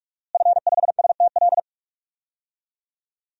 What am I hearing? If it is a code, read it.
Morse code, heard as a keyed tone, about 42 wpm, 700 Hz. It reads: F5STL
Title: DXCC-100 Call Signs - 5 Characters - Repeated 3x - 45wpm